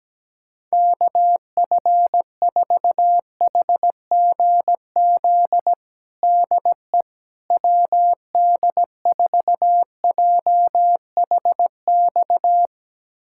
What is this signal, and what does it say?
Morse code, heard as a keyed tone, about 17 wpm, 700 Hz. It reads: KF4HGZ DE WD4JHX
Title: USA Callsign DE Callsign - 6 Characters - 17wpm